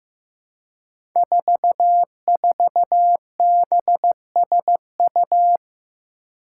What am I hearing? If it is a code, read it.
Morse code, heard as a keyed tone, about 15 wpm, 700 Hz. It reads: 44BSU
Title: Five Challenging Letter-Number Combinations 15wpm